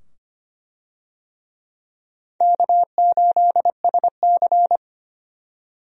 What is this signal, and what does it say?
Morse code, heard as a keyed tone, about 25 wpm, 700 Hz. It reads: K8SC
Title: USA Call Signs - 4 Characters - 25wpm